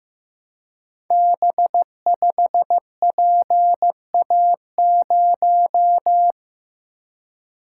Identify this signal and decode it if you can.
Morse code, heard as a keyed tone, about 15 wpm, 700 Hz. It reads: B5PA0
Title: Code Groups - 5 Letter:Numbers - 15wpm